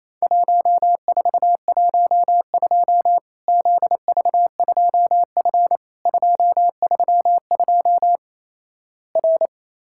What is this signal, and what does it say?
Morse code, heard as a keyed tone, about 28 wpm, 700 Hz. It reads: 1412 ZV2F 232 R
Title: Worked All Europe DX Contest - QTC Exchange - 28wpm